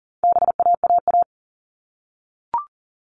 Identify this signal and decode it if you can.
Morse code, heard as a keyed tone, about 40 wpm, 700 Hz. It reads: BAAA
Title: All Cut Numbers - 4-Digit 40wpm